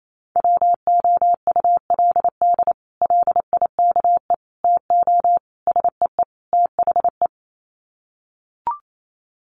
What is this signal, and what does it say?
Morse code, heard as a keyed tone, about 28 wpm, 700 Hz. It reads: WOULD LIKE TO SEE THE
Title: Sets of 5 Words Encore2 28wpm